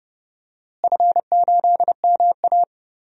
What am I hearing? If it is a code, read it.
Morse code, heard as a keyed tone, about 30 wpm, 700 Hz. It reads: F8MA
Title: DXCC-100 Call Signs - 4 Characters - Repeated 2x - 30wpm